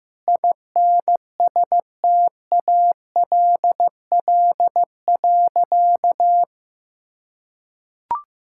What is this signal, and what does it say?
Morse code, heard as a keyed tone, about 15 wpm, 700 Hz. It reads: INSTALL.
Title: Sentences from Top 1600 Words 15wpm